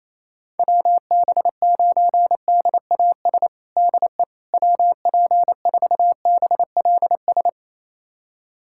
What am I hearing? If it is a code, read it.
Morse code, heard as a keyed tone, about 27 wpm, 700 Hz. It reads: WB9DAS DE WP4BLS
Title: USA Callsign DE Callsign - 6 Characters - 28wpm